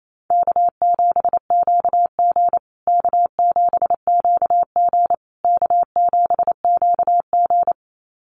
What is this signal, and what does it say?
Morse code, heard as a keyed tone, about 28 wpm, 700 Hz. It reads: K7QG K7QG K7QG
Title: USA Call Signs Repeated 3x 28wpm